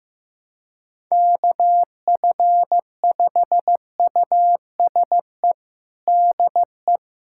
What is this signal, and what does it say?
Morse code, heard as a keyed tone, about 15 wpm, 700 Hz. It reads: KF5USE DE
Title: USA Callsign DE Callsign 15wpm